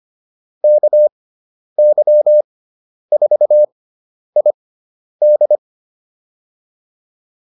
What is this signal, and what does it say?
Morse code, heard as a keyed tone, about 25 wpm, spaced as for 10 wpm, 600 Hz. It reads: KY4ID